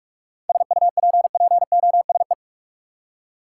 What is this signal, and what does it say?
Morse code, heard as a keyed tone, about 46 wpm, 700 Hz. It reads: SUPPOSE
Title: Seven Letter Words - 45wpm